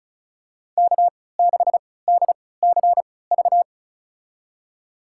Morse code, heard as a keyed tone, about 36 wpm, spaced as for 20 wpm, 700 Hz. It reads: K6DCV
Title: USA Call Signs - 5 Characters - 3x Character Spacing - 35wpm